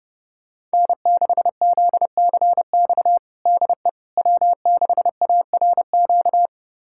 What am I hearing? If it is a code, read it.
Morse code, heard as a keyed tone, about 30 wpm, 700 Hz. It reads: N6ZCX DE W6ARQ
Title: USA Callsign DE Callsign - 5 Characters - 30wpm